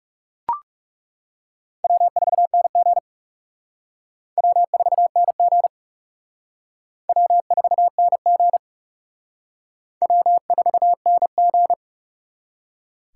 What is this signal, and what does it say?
Morse code, heard as a keyed tone, about 35 wpm, 700 Hz. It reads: W4NG W4NG W4NG W4NG
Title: Speed Racing - USA 1x2 Call Signs 30wpm